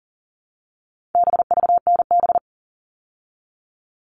Morse code, heard as a keyed tone, about 40 wpm, 700 Hz. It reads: BVNB
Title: All Cut Numbers - 4-Digit 40wpm